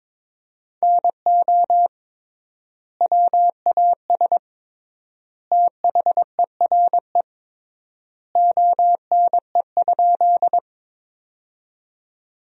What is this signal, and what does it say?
Morse code, heard as a keyed tone, about 21 wpm, 700 Hz. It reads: NO WAS THERE ONE?
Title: 3x Word-Spacing - Sentences from Top 100 Words - 22wpm